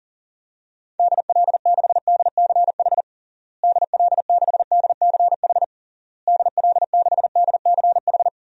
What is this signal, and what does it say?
Morse code, heard as a keyed tone, about 40 wpm, 700 Hz. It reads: DL6DCH DL6DCH DL6DCH
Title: DXCC-100 Call Signs - 6 Characters - Repeated 3x - 40wpm